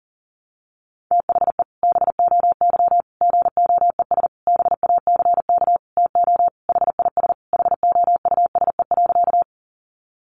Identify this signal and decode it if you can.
Morse code, heard as a keyed tone, about 40 wpm, 700 Hz. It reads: THE BOY GOES BACK TO HIS HOUSE.
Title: Easy Phrases 40wpm